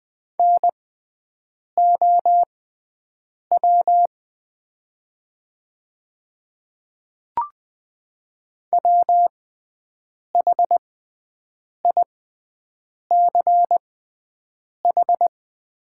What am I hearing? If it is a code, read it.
Morse code, heard as a keyed tone, about 20 wpm, spaced as for 7 wpm, 690 Hz. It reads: NOW WHICH
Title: 6x Character Spacing - Top 100 Words - 20wpm